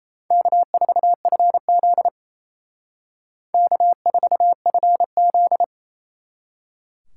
Morse code, heard as a keyed tone, about 35 wpm, 700 Hz. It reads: K4FZ K4FZ